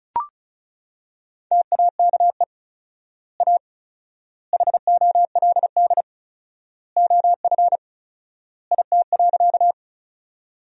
Morse code, heard as a keyed tone, about 36 wpm, 700 Hz. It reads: TAKE A HOLD OF IT.
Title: 4x Word-Spacing - Sentences from Top 400 Words - 35wpm